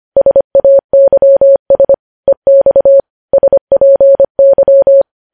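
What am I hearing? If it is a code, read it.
Morse code, heard as a keyed tone, about 25 wpm, 550 Hz. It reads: SAYS EX SPY
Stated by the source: Morse code news headlines at 25 wpm.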